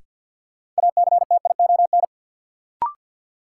Morse code, heard as a keyed tone, about 52 wpm, 700 Hz. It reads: ACTION
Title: Sentences from Top 700 Words - New Word Drill - 50wpm